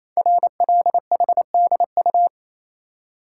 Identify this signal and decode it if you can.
Morse code, heard as a keyed tone, about 29 wpm, 700 Hz. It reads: RLHDU
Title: Code Groups - 5 Letters - 28wpm